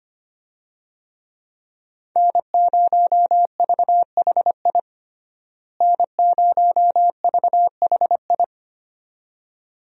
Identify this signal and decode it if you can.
Morse code, heard as a keyed tone, about 25 wpm, 700 Hz. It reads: N0VHI N0VHI
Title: USA Call Signs - 5 Characters - Repeated 2x - 25wpm